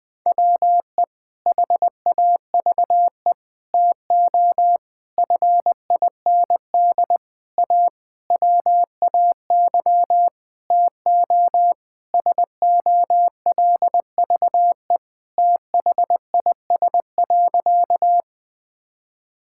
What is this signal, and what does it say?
Morse code, heard as a keyed tone, about 20 wpm, 700 Hz. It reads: WE HAVE TO FIND A WAY TO SOLVE THIS.
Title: Sentences with 6-Word Phrases - 20wpm